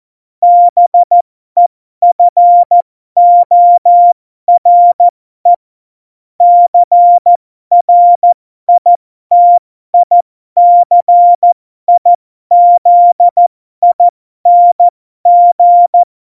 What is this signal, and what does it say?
Morse code, heard as a keyed tone, about 14 wpm, spaced as for 12 wpm, 700 Hz. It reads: BEFORE CRITICIZING
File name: Morsekurs / Witze